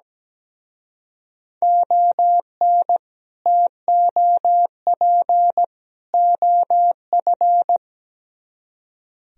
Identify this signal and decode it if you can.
Morse code, heard as a keyed tone, about 20 wpm, 700 Hz. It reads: ON TOP OF
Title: Speed-Racing - Sets of 3 Words 17wpm